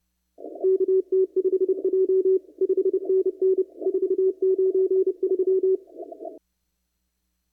Morse code, heard as a keyed tone, about 31 wpm, 380 Hz. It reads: KT5J 5NN 493
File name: CW / kt5j wpx cw-op-k5pi